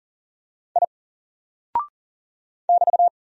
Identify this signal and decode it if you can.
Morse code, heard as a keyed tone, about 39 wpm, 700 Hz. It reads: I <BT>
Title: Single Character (Letter:Number:Punctuation:Prosign) 40wpm